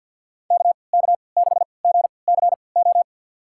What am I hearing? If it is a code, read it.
Morse code, heard as a keyed tone, about 53 wpm, spaced as for 29 wpm, 700 Hz. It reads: KK6GCO
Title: USA Call Signs - 6 Characters - 3x Character Spacing - 50wpm